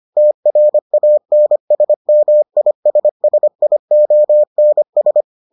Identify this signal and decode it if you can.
Morse code, heard as a keyed tone, about 25 wpm, 600 Hz. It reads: TRANSMISSIONS